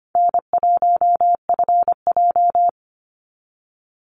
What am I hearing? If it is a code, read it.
Morse code, heard as a keyed tone, about 25 wpm, 700 Hz. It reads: N1FJ
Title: NA SOTA Activator Callsigns 25wpm